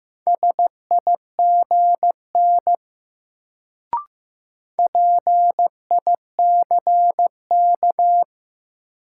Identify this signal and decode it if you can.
Morse code, heard as a keyed tone, about 16 wpm, 700 Hz. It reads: SIGN PICK
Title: Sentences from Top 1000 Words - New Word Drill - 15wpm